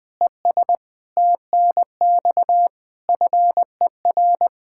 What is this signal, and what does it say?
Morse code, heard as a keyed tone, about 20 wpm, 700 Hz. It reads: ES TNX FER
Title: Practice QSO - Exchange 2 - Info - 20wpm